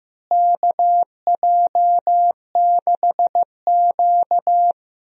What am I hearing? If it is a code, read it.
Morse code, heard as a keyed tone, about 15 wpm, 700 Hz. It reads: KJ6Q
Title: USA Call Signs - 4 Characters - Repeated 3x - 15wpm